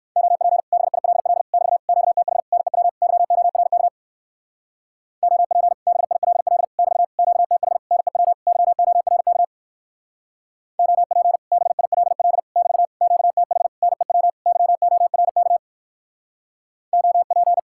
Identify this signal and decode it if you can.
Morse code, heard as a keyed tone, about 57 wpm, 700 Hz. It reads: OP BILL <BT> QTH NEW YORK OP BILL <BT> QTH NEW YORK OP BILL <BT> QTH NEW YORK OP